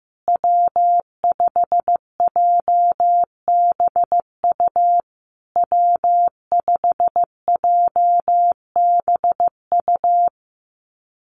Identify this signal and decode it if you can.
Morse code, heard as a keyed tone, about 15 wpm, 700 Hz. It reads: W5JBU W5JBU
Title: USA Call Signs Repeated 3x 15wpm